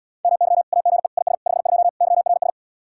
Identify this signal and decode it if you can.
Morse code, heard as a keyed tone, about 80 wpm, 700 Hz. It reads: MY AGE IS 52 YRS